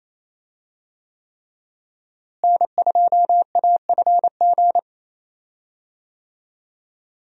Morse code, heard as a keyed tone, about 28 wpm, 700 Hz. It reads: N2AFG
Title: USA Call Signs - 5 Characters - Repeated 3x - 28wpm